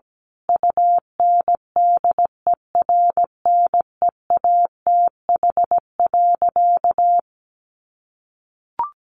Morse code, heard as a keyed tone, about 17 wpm, 700 Hz. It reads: UNDERNEATH.
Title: A Sound Of Thunder 17wpm